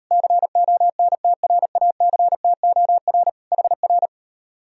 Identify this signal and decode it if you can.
Morse code, heard as a keyed tone, about 38 wpm, 700 Hz. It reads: CONTRACTOR HR